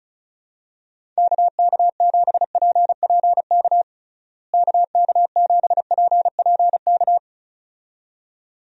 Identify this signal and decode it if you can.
Morse code, heard as a keyed tone, about 34 wpm, 700 Hz. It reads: KK7PPK KK7PPK
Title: USA Call Signs - 6 Characters - Repeated 2x - 35wpm